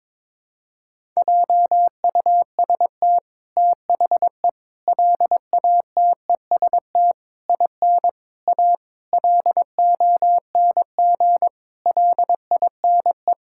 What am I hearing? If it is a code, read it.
Morse code, heard as a keyed tone, about 22 wpm, 700 Hz. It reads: JUST THE LATEST IN A LONG LINE
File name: practice / 22wpm / Sets of 8 Words - 22wpm